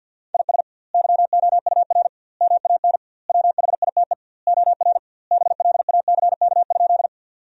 Morse code, heard as a keyed tone, about 50 wpm, 700 Hz. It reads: IS YOUR MAN WHITE OR BLACK?
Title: 2x Word-Spacing - Sentences from Top 400 Words - 50wpm